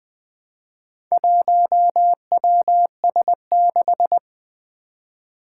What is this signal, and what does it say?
Morse code, heard as a keyed tone, about 20 wpm, 700 Hz. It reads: 1WS6
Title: Four Challenging Letter-Number Combinations 20wpm